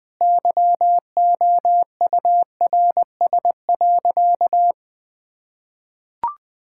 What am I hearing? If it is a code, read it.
Morse code, heard as a keyed tone, about 21 wpm, 700 Hz. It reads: YOURS.
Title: Sentences with 8-Word Phrases - Spoken Beginning - 20wpm